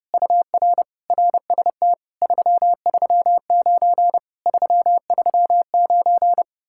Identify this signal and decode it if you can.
Morse code, heard as a keyed tone, about 30 wpm, 700 Hz. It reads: UR RST 339 339